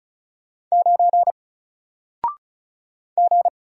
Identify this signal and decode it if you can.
Morse code, heard as a keyed tone, about 36 wpm, 700 Hz. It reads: 9 G
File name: Single-Letter-Number-35wpm